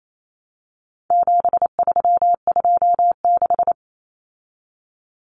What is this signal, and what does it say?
Morse code, heard as a keyed tone, about 28 wpm, 700 Hz. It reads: 7326
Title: Common Cut Numbers - 4-Digit 28wpm